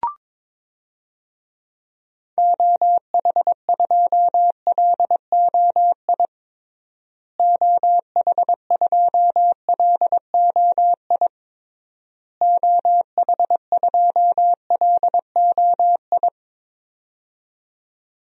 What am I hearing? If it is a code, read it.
Morse code, heard as a keyed tone, about 22 wpm, 700 Hz. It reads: OH2LOI OH2LOI OH2LOI
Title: DXCC-100 Call Signs - 6 Characters - Repeated 3x - 22wpm